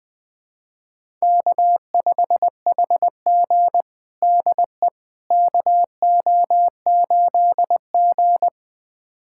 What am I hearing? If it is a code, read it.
Morse code, heard as a keyed tone, about 20 wpm, 700 Hz. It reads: K5HG DE KO8G
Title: USA Callsign DE Callsign - 4 Characters - 20wpm